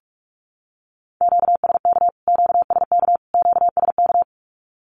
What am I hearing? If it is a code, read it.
Morse code, heard as a keyed tone, about 44 wpm, 700 Hz. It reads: QSK QSK QSK
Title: Q-Codes Repeated 3x 45wpm